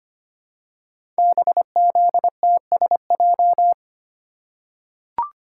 Morse code, heard as a keyed tone, about 25 wpm, 700 Hz. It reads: BZTSJ
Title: Code Groups - 5 Letters - 25wpm